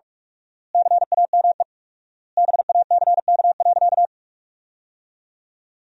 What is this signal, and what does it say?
Morse code, heard as a keyed tone, about 47 wpm, 700 Hz. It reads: CAME BACK.
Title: 4x Word-Spacing - Sentences from Top 300 Words - 45wpm